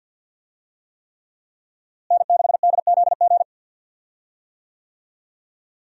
Morse code, heard as a keyed tone, about 52 wpm, 700 Hz. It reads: N6DZG
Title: USA Call Signs - 5 Characters - Repeated 3x - 50wpm